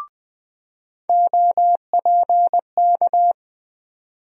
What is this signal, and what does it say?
Morse code, heard as a keyed tone, about 21 wpm, 700 Hz. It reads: OPK
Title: USA Call Sign Postfixes - 3 Letters 20wpm